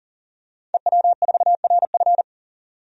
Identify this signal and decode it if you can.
Morse code, heard as a keyed tone, about 40 wpm, 700 Hz. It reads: EW4RF
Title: DXCC-100 Call Signs - 5 Characters - 40wpm